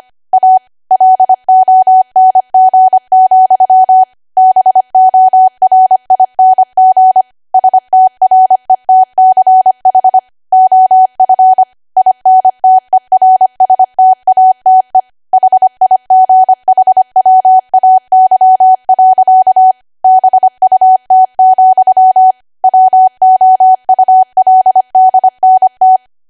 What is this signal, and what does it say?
Morse code, heard as a keyed tone, about 25 wpm, 750 Hz. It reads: A LONG, BORING STRETCH OF INTERSTATE HIGHWAY. BUT, WOULDNT
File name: W1AW-Level3/140218 25WPM